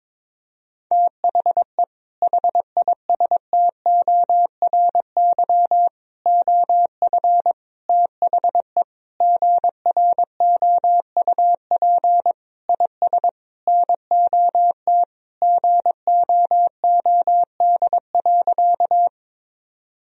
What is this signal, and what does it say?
Morse code, heard as a keyed tone, about 22 wpm, 700 Hz. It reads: THE HISTORY OF THE GROUP IS NOT GOOD.
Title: Sentences from Top 1000 Words - Review Part 1 - 22wpm